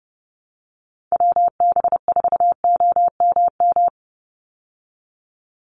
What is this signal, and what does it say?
Morse code, heard as a keyed tone, about 30 wpm, 700 Hz. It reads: WB4OMM
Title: Call Signs Repeated 3x - 30wpm